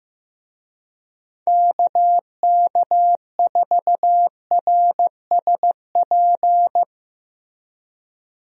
Morse code, heard as a keyed tone, about 15 wpm, 700 Hz. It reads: KK4RSP
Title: USA Call Signs - 6 Characters - Repeated 2x - 15wpm